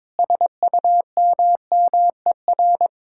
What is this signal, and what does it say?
Morse code, heard as a keyed tone, about 21 wpm, 700 Hz. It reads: SUMMER
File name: Difficult Sentences with Hints - 22wpm